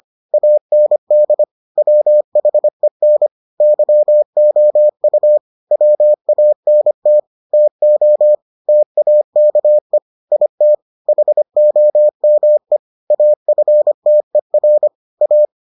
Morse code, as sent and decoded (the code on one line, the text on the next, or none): .- -. -.. / .-- .... -. / -.-- --- ..- / .-- .- -. - / - --- / - .- -.- . / .. - / .... --- -- . / .- ..-. - . .-. / .-
AND WHEN YOU WANT TO TAKE IT HOME AFTER A